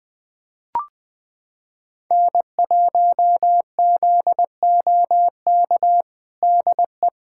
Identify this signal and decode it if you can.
Morse code, heard as a keyed tone, about 20 wpm, 700 Hz. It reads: N1ZOK DE
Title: USA Callsign DE Callsign - 5 Characters - 20wpm